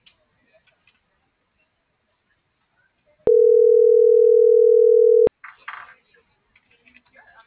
Ringback tone (North America), 440 + 480 Hz.